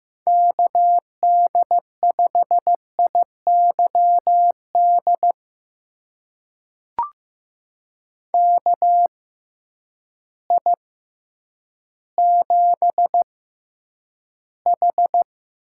Morse code, heard as a keyed tone, about 15 wpm, spaced as for 9 wpm, 700 Hz. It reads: KD5IYD KI7H
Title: USA Call Signs - 6 Characters - 6x Character Spacing - 15wpm